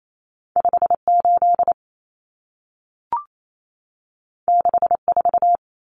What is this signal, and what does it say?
Morse code, heard as a keyed tone, about 28 wpm, 700 Hz. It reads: 58 64
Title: Common Cut Numbers - 2-Digit 28wpm